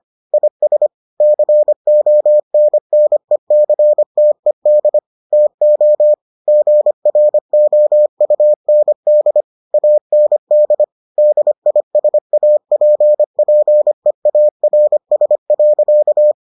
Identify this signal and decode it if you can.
Morse code, heard as a keyed tone, about 25 wpm, 600 Hz. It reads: IS CONNECTED TO GROUND AND DISAPPEARS.